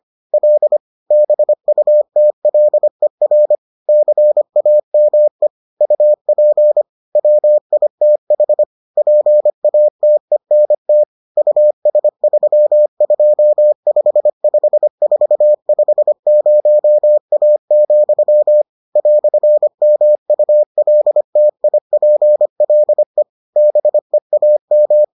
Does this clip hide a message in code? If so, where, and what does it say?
Yes: Morse code from 0.3 to 25.0 s, heard as a keyed tone, about 25 wpm, 600 Hz. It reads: L BUTLER CAME UP WITH PATENT US3255450A, "MULTIPLE BEAM